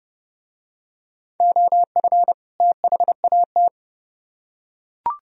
Morse code, heard as a keyed tone, about 30 wpm, 700 Hz. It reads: OF THAT